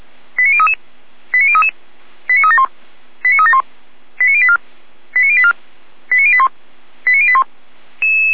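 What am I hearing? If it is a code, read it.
Five-tone selective call (ZVEI), heard as quick runs of five tones, tones of about 70 ms. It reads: 89033 89033 89371 89371 89084 89084 89081 89081
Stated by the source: ZVEI five-tone selective call 89169 received on SDR, from the feuerwehr-orlamuende.org fire brigade audio archive.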